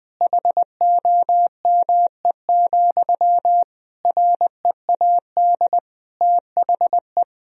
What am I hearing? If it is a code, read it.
Morse code, heard as a keyed tone, about 21 wpm, 700 Hz. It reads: HOME, READ THE
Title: Sentences from Top 1000 Words - Review Part 2 - 20wpm